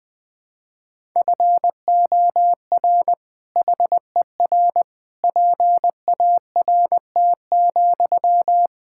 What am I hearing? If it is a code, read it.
Morse code, heard as a keyed tone, about 19 wpm, 700 Hz. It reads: FOR HER PART,
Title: Sentences from Top 1000 Words - Review Part 5 - 20wpm